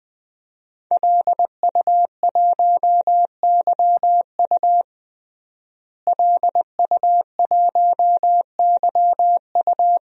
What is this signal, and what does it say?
Morse code, heard as a keyed tone, about 20 wpm, 700 Hz. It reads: LU1YU LU1YU
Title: DXCC-100 Call Signs - 5 Characters - Repeated 2x - 20wpm